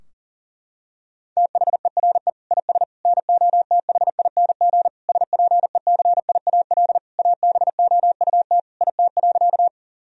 Morse code, heard as a keyed tone, about 37 wpm, 700 Hz. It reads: THERE IS NOTHING SPECIAL ABOUT IT.
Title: Sentences from Top 1000 Words - Review Part 3 - 40wpm